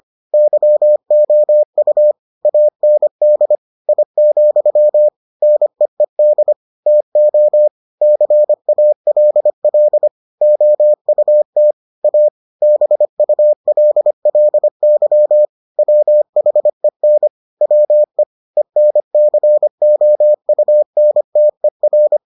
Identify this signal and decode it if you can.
Morse code, heard as a keyed tone, about 25 wpm, 600 Hz. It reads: YOU AND I, NEED TO CALL OUT A BULLY WHEN WE ENCOUNTER